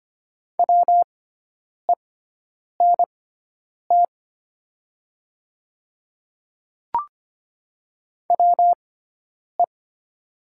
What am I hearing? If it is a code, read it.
Morse code, heard as a keyed tone, about 25 wpm, spaced as for 9 wpm, 700 Hz. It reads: WENT WE
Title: 6x Character Spacing - Top 500 Words - 25wpm